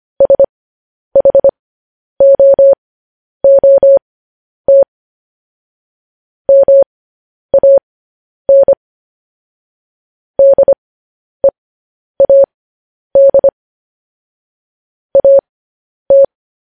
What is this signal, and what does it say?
Morse code, heard as a keyed tone, about 25 wpm, spaced as for 10 wpm, 550 Hz. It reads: SHOOT MAN DEAD AT
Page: news headlines in morse code at 10 wpm